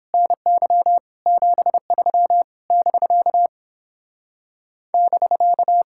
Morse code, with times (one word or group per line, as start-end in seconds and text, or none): 0.1-1.0 NY
1.3-2.4 73
2.7-3.5 <BK>
4.9-5.8 <BK>